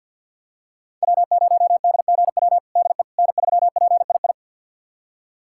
Morse code, heard as a keyed tone, about 51 wpm, 700 Hz. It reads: W0DGW DE N3PII
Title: USA Callsign DE Callsign - 5 Characters - 50wpm